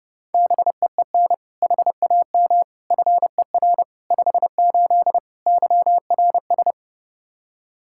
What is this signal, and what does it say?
Morse code, heard as a keyed tone, about 30 wpm, 700 Hz. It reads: BEEN HAM FER 58 YRS
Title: Practice QSO - Exchange 3 - More Info - 30wpm